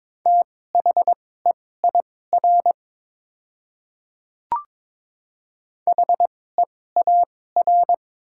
Morse code, heard as a keyed tone, about 22 wpm, spaced as for 16 wpm, 690 Hz. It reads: THEIR HEAR
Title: 2x Character Spacing - Top 300 Words - 22wpm